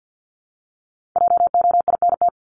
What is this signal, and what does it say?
Morse code, heard as a keyed tone, about 50 wpm, 700 Hz. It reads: JOINT
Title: Top 1251-1500 Words 50wpm